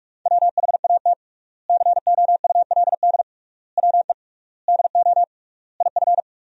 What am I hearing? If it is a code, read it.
Morse code, heard as a keyed tone, about 43 wpm, 700 Hz. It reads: WHAT COULD WE DO IF
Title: 3x Word-Spacing - Sentences from Top 400 Words - 45wpm